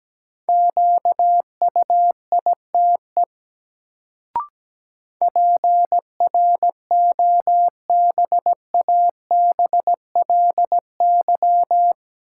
Morse code, heard as a keyed tone, about 17 wpm, 700 Hz. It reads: QUITE PROBABLY
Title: Sentences from Top 700 Words - New Word Drill - 17wpm